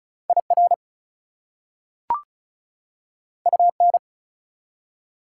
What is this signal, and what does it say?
Morse code, heard as a keyed tone, about 36 wpm, 700 Hz. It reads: IR UN
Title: Prefixes and Suffixes 35wpm